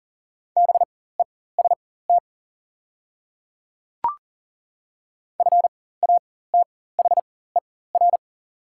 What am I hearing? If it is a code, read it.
Morse code, heard as a keyed tone, about 40 wpm, spaced as for 19 wpm, 690 Hz. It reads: BEST FATHER
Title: 4x Character Spacing - Top 500 Words - 40wpm